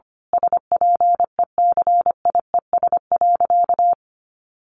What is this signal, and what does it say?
Morse code, heard as a keyed tone, about 25 wpm, 700 Hz. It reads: SPECIES.
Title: A Sound Of Thunder 25wpm